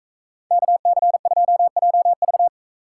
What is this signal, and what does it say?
Morse code, heard as a keyed tone, about 42 wpm, 700 Hz. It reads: KC2JV